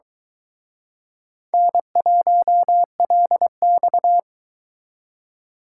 Morse code, heard as a keyed tone, about 23 wpm, 700 Hz. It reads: N1LX